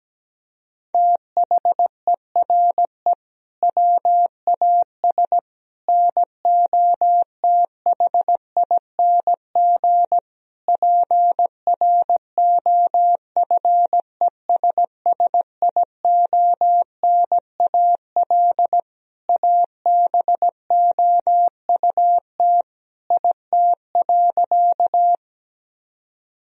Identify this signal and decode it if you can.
Morse code, heard as a keyed tone, about 17 wpm, 700 Hz. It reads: THERE WAS NOTHING PROFESSIONAL ABOUT IT.
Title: Sentences from Top 1000 Words - Review Part 4 - 17wpm